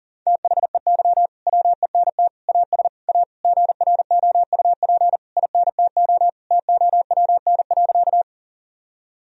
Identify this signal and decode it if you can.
Morse code, heard as a keyed tone, about 40 wpm, 700 Hz. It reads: THEY WENT AS A GROUP INTO TOWN.
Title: Sentences from Top 1000 Words - Review Part 1 - 40wpm